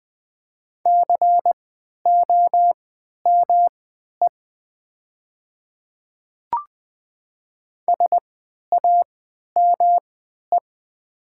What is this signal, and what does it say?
Morse code, heard as a keyed tone, about 20 wpm, spaced as for 11 wpm, 700 Hz. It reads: COME SAME
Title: 3x Character Spacing - Top 200 Words - 20wpm